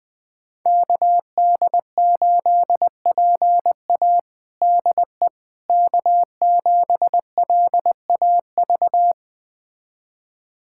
Morse code, heard as a keyed tone, about 21 wpm, 700 Hz. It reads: KD8PA DE K7LAV
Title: USA Callsign DE Callsign - 5 Characters - 20wpm